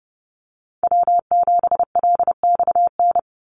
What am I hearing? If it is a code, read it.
Morse code, heard as a keyed tone, about 30 wpm, 700 Hz. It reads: W7LXN
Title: Call Signs Repeated 3x - 30wpm